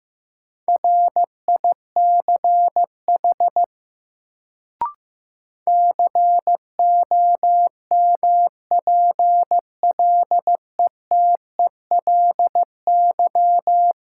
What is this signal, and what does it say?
Morse code, heard as a keyed tone, about 16 wpm, 700 Hz. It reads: RICH COMPLETELY
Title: Sentences from Top 1100 Words - New Word Drill Alt - 15wpm